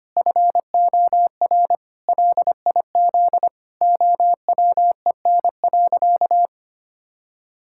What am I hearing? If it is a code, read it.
Morse code, heard as a keyed tone, about 25 wpm, 700 Hz. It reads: FOR LIZ OWEN.
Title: Pangram Challenge v2 - 25wpm